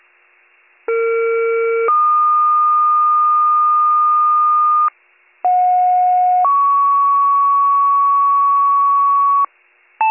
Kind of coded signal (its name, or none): two-tone page